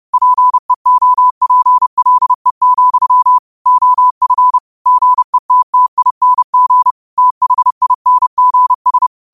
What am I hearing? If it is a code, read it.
Morse code, heard as a keyed tone, about 30 wpm, 1000 Hz. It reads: PEOPLE, OF GETTING THINGS
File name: qotdfiles / 30wpm